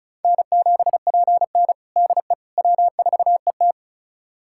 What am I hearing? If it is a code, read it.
Morse code, heard as a keyed tone, about 36 wpm, 700 Hz. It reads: N7PN DE W4ET
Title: USA Callsign DE Callsign - 4 Characters - 35wpm